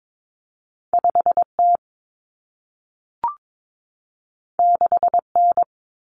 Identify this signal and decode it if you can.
Morse code, heard as a keyed tone, about 22 wpm, 700 Hz. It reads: 5T 6N
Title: Common Cut Numbers - 2-Digit 22wpm